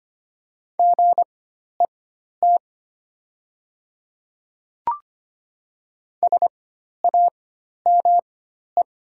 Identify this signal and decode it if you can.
Morse code, heard as a keyed tone, about 25 wpm, spaced as for 12 wpm, 700 Hz. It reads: GET SAME
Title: 4x Character Spacing - Top 200 Words - 25wpm